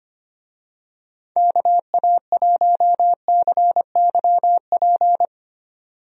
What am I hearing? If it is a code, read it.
Morse code, heard as a keyed tone, about 25 wpm, 700 Hz. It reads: KA1CYP
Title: USA Call Signs - 6 Characters - 25wpm